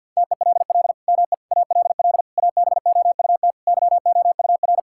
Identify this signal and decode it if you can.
Morse code, heard as a keyed tone, about 47 wpm, 700 Hz. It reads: TELL ME ALL ABOUT YOUR
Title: Sentences from Top 1000 Words - Review Part 6 - 50wpm